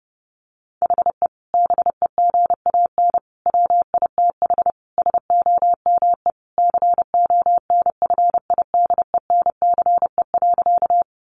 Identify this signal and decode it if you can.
Morse code, heard as a keyed tone, about 30 wpm, 700 Hz. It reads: HE BEGAN WITH SOME CONFIDENCE.
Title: Sentences from Top 1500 Words 30wpm